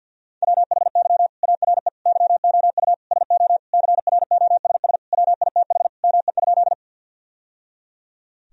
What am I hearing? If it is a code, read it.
Morse code, heard as a keyed tone, about 50 wpm, 700 Hz. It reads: WHY ARE YOU SO CROSS WITH ME?
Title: Sentences from Top 1000 Words - Review Part 1 - 50wpm